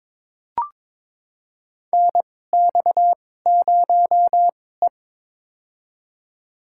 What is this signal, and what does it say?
Morse code, heard as a keyed tone, about 22 wpm, spaced as for 16 wpm, 700 Hz. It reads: NX0E
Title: USA Call Signs - 4 Characters - 2x Character Spacing - 22wpm